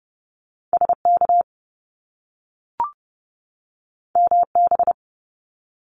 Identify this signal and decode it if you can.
Morse code, heard as a keyed tone, about 30 wpm, 700 Hz. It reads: SK MB
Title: Top Two Letter Combinations in Words 30wpm